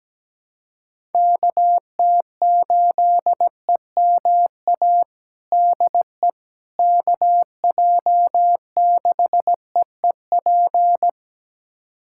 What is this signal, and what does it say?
Morse code, heard as a keyed tone, about 17 wpm, 700 Hz. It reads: KT8EMA DE KJ6EEP
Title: USA Callsign DE Callsign 17wpm